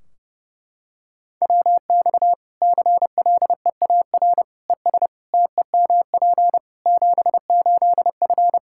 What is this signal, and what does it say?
Morse code, heard as a keyed tone, about 32 wpm, 700 Hz. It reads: WX CLEAR ES TEMP 78F
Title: Practice QSO - Exchange 2 - Info - 30wpm